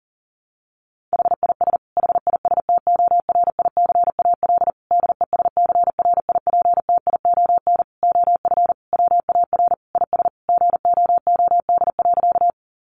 Morse code, heard as a keyed tone, about 40 wpm, 700 Hz. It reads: HIS HISTORICAL DESCRIPTION OF WAR IS GOOD.
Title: Sentences from Top 1300 Words 40wpm